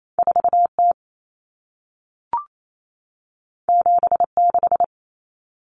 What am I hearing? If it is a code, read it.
Morse code, heard as a keyed tone, about 28 wpm, 700 Hz. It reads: 4T 76
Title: Common Cut Numbers - 2-Digit 28wpm